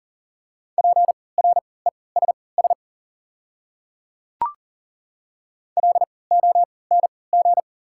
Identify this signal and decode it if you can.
Morse code, heard as a keyed tone, about 39 wpm, spaced as for 22 wpm, 700 Hz. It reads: PRESS LONG
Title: 3x Character Spacing - Top 300 Words - 40wpm